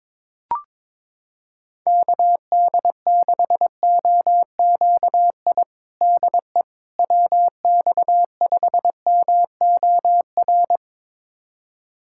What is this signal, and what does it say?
Morse code, heard as a keyed tone, about 22 wpm, 700 Hz. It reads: KD6OQI DE WX5MOR
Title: USA Callsign DE Callsign - 6 Characters - 22wpm